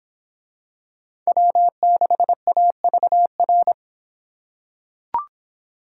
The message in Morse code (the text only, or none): W6AVR